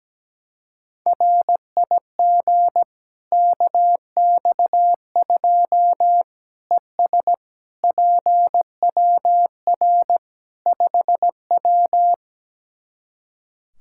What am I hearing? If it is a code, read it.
Morse code, heard as a keyed tone, about 17 wpm, 700 Hz. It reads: RIG KX2 ES PWR 5W
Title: K1USN SST QSO Practice 17wpm